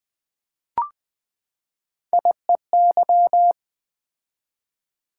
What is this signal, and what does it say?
Morse code, heard as a keyed tone, about 20 wpm, 700 Hz. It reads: IEY
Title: Three Letter-Number Combinations - 20wpm